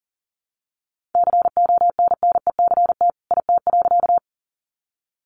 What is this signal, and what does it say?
Morse code, heard as a keyed tone, about 40 wpm, 700 Hz. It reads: CONNECT IT.
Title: Sentences from Top 1500 Words 40wpm